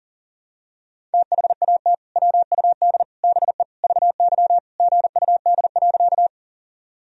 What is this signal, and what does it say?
Morse code, heard as a keyed tone, about 40 wpm, 700 Hz. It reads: THAT WUD BE VY GUD.
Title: Sentences from Top 200 Words and Abbreviations 40wpm